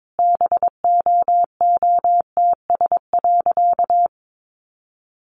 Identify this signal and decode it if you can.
Morse code, heard as a keyed tone, about 22 wpm, 700 Hz. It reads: BOOTS.
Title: A Sound Of Thunder 22wpm